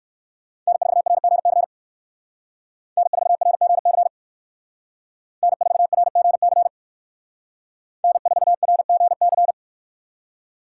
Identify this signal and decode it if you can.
Morse code, heard as a keyed tone, about 58 wpm, 700 Hz. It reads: N4RGC N4RGC N4RGC N4RGC